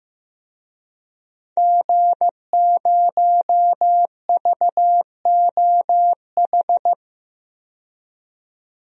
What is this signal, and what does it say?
Morse code, heard as a keyed tone, about 15 wpm, 700 Hz. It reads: G0VOH
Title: DXCC-100 Call Signs - 5 Characters - 15wpm